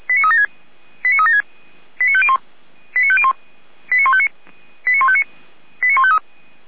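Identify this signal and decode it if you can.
Five-tone selective call (ZVEI), heard as quick runs of five tones, tones of about 70 ms. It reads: 89376 89501 89501 89159 89159 89153
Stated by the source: ZVEI five-tone selective call received on an SDR.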